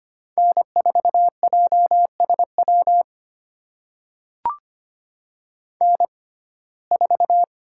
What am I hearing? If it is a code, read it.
Morse code, heard as a keyed tone, about 25 wpm, spaced as for 18 wpm, 700 Hz. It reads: N4JSW N4
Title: USA Call Signs - 5 Characters - 6x Character Spacing - 25wpm